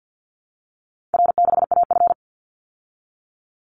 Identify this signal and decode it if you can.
Morse code, heard as a keyed tone, about 50 wpm, 700 Hz. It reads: R6AF
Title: Easy Call Signs Repeated 3x - 50wpm